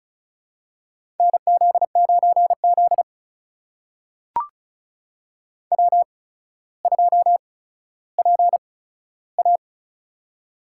Morse code, heard as a keyed tone, about 36 wpm, spaced as for 15 wpm, 700 Hz. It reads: NZ9Z W2PA
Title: USA Call Signs - 4 Characters - 8x Character Spacing - 35wpm